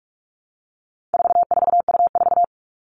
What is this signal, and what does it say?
Morse code, heard as a keyed tone, about 44 wpm, 700 Hz. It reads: 44U4
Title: All Cut Numbers - 4-Digit 45wpm